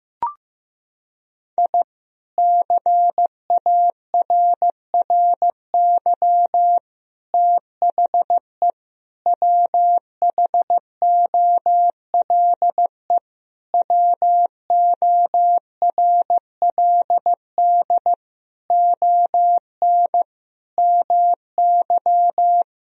Morse code, heard as a keyed tone, about 14 wpm, 700 Hz. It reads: I CARRY THE WHOLE WORLD ON MY
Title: Sentences from Top 1000 Words - Review Part 1 - 15wpm